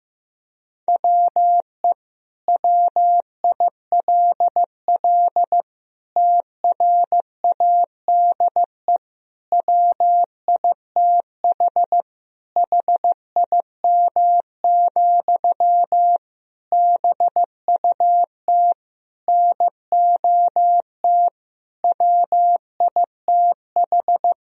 Morse code, heard as a keyed tone, about 15 wpm, 700 Hz. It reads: WE WILL TRADE WITH HIM, BUT NOT WITH
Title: Sentences from Top 1000 Words - Review Part 4 - 15wpm